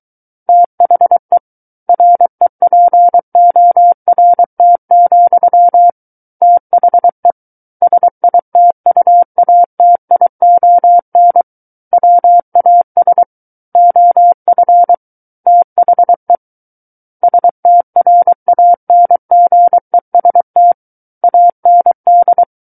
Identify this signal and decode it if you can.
Morse code, heard as a keyed tone, about 23 wpm, 700 Hz. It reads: THE REPORT, THE SITUATION WAS OF THE STRANGEST AND
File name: Book One / 14-Chapter 14 - 23 WPM